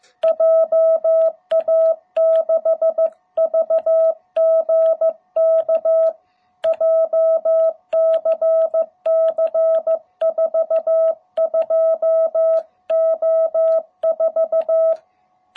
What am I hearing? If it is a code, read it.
Morse code, heard as a keyed tone, about 15 wpm, 660 Hz. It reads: JA6VGK JCC 42O4